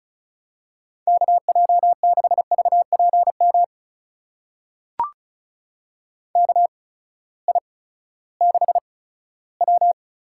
Morse code, heard as a keyed tone, about 36 wpm, spaced as for 18 wpm, 700 Hz. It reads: KJ6VPM KI6W
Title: USA Call Signs - 6 Characters - 8x Character Spacing - 35wpm